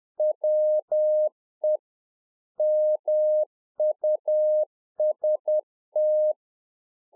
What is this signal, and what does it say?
Morse code, heard as a keyed tone, about 10 wpm, 610 Hz. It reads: WE MUST